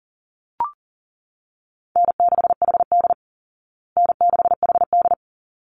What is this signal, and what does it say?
Morse code, heard as a keyed tone, about 40 wpm, 700 Hz. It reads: N6HD N6HD
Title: Easy Call Signs Repeated 2x - 40wpm